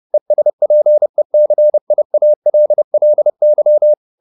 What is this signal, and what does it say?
Morse code, heard as a keyed tone, about 30 wpm, 600 Hz. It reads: ESPECIALLY
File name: the old man and the sea / the old man and the sea 1 30wpm